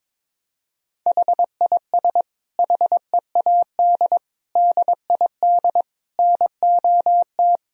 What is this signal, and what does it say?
Morse code, heard as a keyed tone, about 22 wpm, 700 Hz. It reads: HIS HEAD DID NOT
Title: Sentences from Top 1000 Words - Review Part 2 - 22wpm